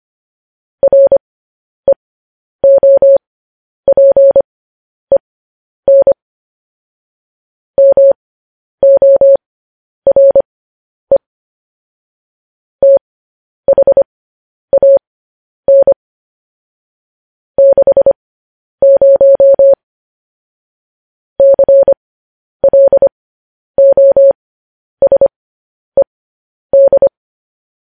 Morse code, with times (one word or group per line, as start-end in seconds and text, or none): 0.8-6.1 REOPEN
7.8-11.2 MORE
12.8-15.9 THAN
17.6-19.7 60
21.4-27.1 CLOSED